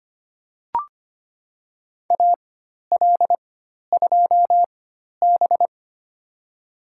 Morse code, heard as a keyed tone, about 25 wpm, spaced as for 12 wpm, 700 Hz. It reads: AL2B
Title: USA Call Signs - 4 Characters - 4x Character Spacing - 25wpm